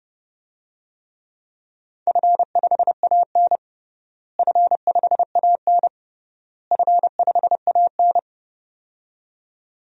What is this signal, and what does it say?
Morse code, heard as a keyed tone, about 30 wpm, 700 Hz. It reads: F5AN F5AN F5AN
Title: DXCC-100 Call Signs - 4 Characters - Repeated 3x - 30wpm